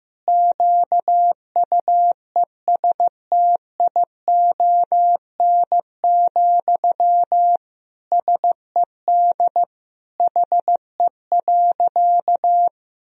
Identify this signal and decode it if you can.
Morse code, heard as a keyed tone, about 15 wpm, 700 Hz. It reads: QUESTION, SED HE.
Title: Sentences from Top 400 Words and Abbreviations 15wpm